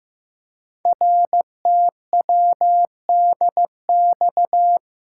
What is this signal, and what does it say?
Morse code, heard as a keyed tone, about 15 wpm, 700 Hz. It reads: RTWDX
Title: Code Groups - 5 Letters - 15wpm